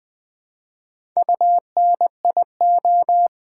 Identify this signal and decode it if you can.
Morse code, heard as a keyed tone, about 20 wpm, 700 Hz. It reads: UNIO
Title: Build A Word - 1 Letter At A Time - 3 to 5-Letter Words - 20wpm